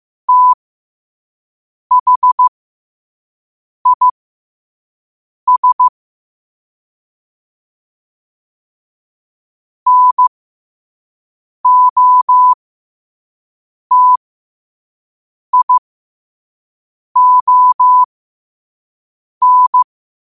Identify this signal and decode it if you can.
Morse code, heard as a keyed tone, about 15 wpm, spaced as for 5 wpm, 1000 Hz. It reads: THIS NOTION
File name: qotdfiles / 5wpm